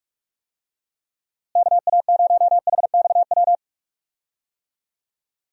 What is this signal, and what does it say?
Morse code, heard as a keyed tone, about 46 wpm, 700 Hz. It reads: KA0HXW
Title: USA Call Signs - 6 Characters - Repeated 2x - 45wpm